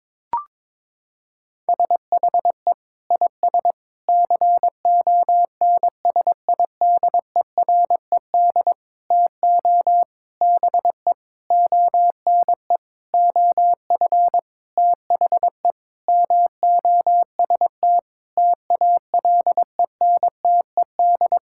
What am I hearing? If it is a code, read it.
Morse code, heard as a keyed tone, about 22 wpm, 700 Hz. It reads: SHE IS CONSIDERED TO BE ONE OF THE MOST TALENTED
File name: practice / 22wpm / Sentences with 8-Word Phrases - 22wpm